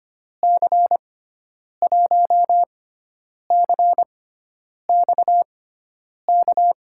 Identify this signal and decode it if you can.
Morse code, heard as a keyed tone, about 25 wpm, spaced as for 9 wpm, 700 Hz. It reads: C1CXK
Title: USA Call Signs - 6 Characters - 6x Character Spacing - 25wpm